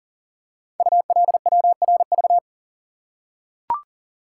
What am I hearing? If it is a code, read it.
Morse code, heard as a keyed tone, about 37 wpm, 700 Hz. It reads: ULWRV
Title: Code Groups - 5 Letters - 40wpm